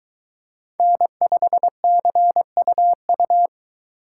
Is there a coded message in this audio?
Yes: Morse code from 0.8 to 3.5 s, heard as a keyed tone, about 23 wpm, 700 Hz. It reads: N5CUU